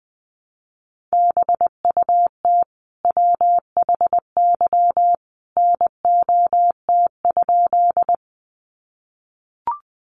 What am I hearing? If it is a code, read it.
Morse code, heard as a keyed tone, about 20 wpm, 700 Hz. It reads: BUT WHY NOT?
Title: Sentences from Top 600 Words 20wpm